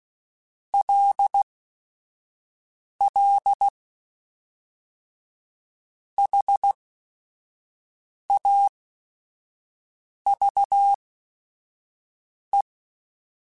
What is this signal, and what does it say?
Morse code, heard as a keyed tone, about 16 wpm, spaced as for 5 wpm, 800 Hz. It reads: LL HAVE
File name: QSOs / QSOs-mp3s / WA7QOX 1@5